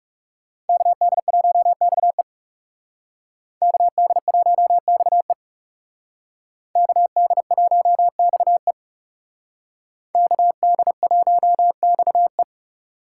Morse code, heard as a keyed tone, about 38 wpm, 700 Hz. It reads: KD1XE KD1XE KD1XE KD1XE